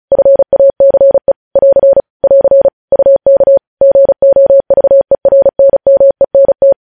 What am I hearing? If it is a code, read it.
Morse code, heard as a keyed tone, about 36 wpm, 550 Hz. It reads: FACE <AR> <AR> UK GOVERNMENT
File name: podcast / News headlines-1x-35x35